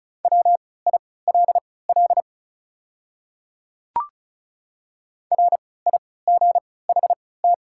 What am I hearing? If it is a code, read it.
Morse code, heard as a keyed tone, about 36 wpm, spaced as for 20 wpm, 700 Hz. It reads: WILL RIGHT
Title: 3x Character Spacing - Top 200 Words - 35wpm